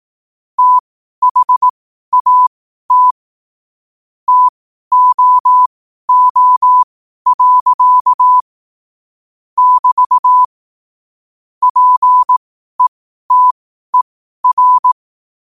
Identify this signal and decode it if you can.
Morse code, heard as a keyed tone, about 18 wpm, spaced as for 12 wpm, 1000 Hz. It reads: THAT TOO. <BT> PETER
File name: qotdfiles / 12wpm